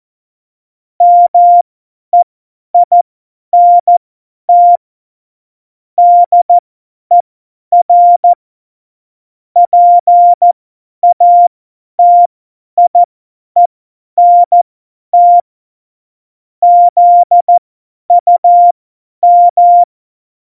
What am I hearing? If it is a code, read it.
Morse code, heard as a keyed tone, about 14 wpm, spaced as for 10 wpm, 700 Hz. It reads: MEINT DER PATIENT ZUM
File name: Morsekurs / Witze / Witz 13. Oktober 2020, 50 F70